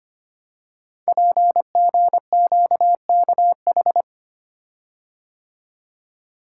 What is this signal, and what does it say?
Morse code, heard as a keyed tone, about 25 wpm, 700 Hz. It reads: PGQKH
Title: Code Groups - 5 Letters - 25wpm